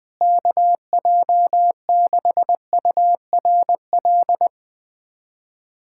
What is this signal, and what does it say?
Morse code, heard as a keyed tone, about 20 wpm, 700 Hz. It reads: KJ6URL
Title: USA Callsign DE Callsign - 6 Characters - 20wpm